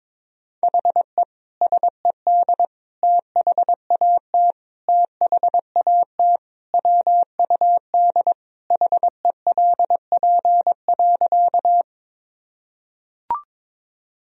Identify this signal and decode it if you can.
Morse code, heard as a keyed tone, about 22 wpm, 700 Hz. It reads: HE SED THAT THAT WUD HELP.
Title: Sentences from Top 200 Words and Abbreviations 22wpm